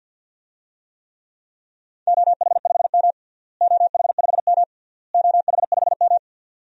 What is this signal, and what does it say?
Morse code, heard as a keyed tone, about 52 wpm, 700 Hz. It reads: OH5M OH5M OH5M
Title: DXCC-100 Call Signs - 4 Characters - Repeated 3x - 50wpm